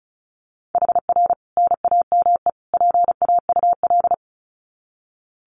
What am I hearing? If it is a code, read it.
Morse code, heard as a keyed tone, about 34 wpm, 700 Hz. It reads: HR NAME PAUL
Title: Common QSO Phrases 35wpm